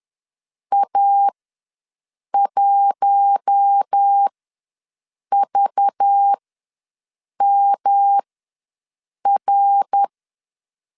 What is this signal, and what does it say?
Morse code, heard as a keyed tone, about 10 wpm, spaced as for 6 wpm, 800 Hz. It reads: A1VMR
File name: LEZIONI / 5 LBVC / 5 LBVC ws 10wpm 3733